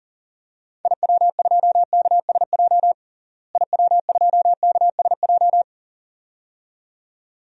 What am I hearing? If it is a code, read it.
Morse code, heard as a keyed tone, about 40 wpm, 700 Hz. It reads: IW2KSJ IW2KSJ
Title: DXCC-100 Call Signs - 6 Characters - Repeated 2x - 40wpm